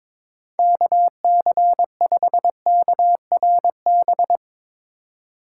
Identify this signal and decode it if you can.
Morse code, heard as a keyed tone, about 22 wpm, 700 Hz. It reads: KC5KRB
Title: USA Call Signs - 6 Characters - 8x Character Spacing - 22wpm